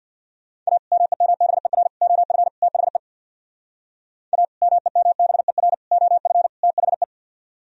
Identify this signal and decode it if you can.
Morse code, heard as a keyed tone, about 64 wpm, 700 Hz. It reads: A MEMBER OF THE A MEMBER OF THE